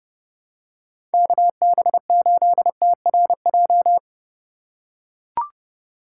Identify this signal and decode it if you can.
Morse code, heard as a keyed tone, about 28 wpm, 700 Hz. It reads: KB8TRJ